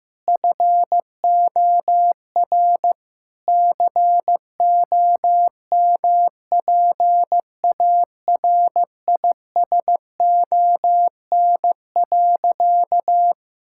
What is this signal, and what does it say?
Morse code, heard as a keyed tone, about 15 wpm, 700 Hz. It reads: FOR COMPARISON.
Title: Sentences from Top 1600 Words 15wpm